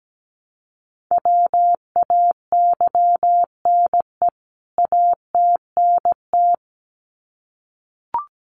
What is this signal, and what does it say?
Morse code, heard as a keyed tone, about 17 wpm, 700 Hz. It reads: WAYNE ATNT
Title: CWT Exchange 17wpm